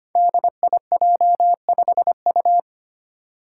Morse code, heard as a keyed tone, about 25 wpm, 700 Hz. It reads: DIJ5U
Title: Five Challenging Letter-Number Combinations 25wpm